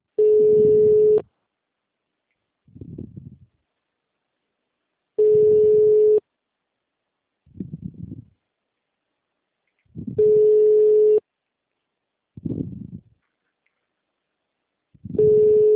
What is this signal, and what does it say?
Ringback tone (Europe, 425 Hz).